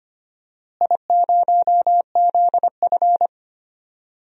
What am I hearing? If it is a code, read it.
Morse code, heard as a keyed tone, about 25 wpm, 700 Hz. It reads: I0ZF
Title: DXCC-100 Call Signs - 4 Characters - Repeated 3x - 25wpm